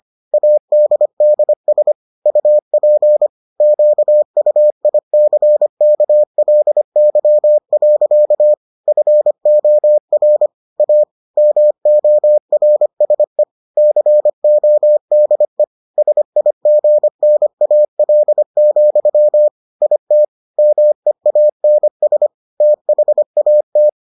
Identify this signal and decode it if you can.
Morse code, heard as a keyed tone, about 25 wpm, 600 Hz. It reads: ADDS UP QUICKLY. FOR A MORSE CODE SIGNAL, IT MEANS THAT